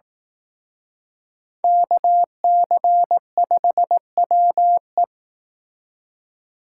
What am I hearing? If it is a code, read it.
Morse code, heard as a keyed tone, about 19 wpm, 700 Hz. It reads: KC5WE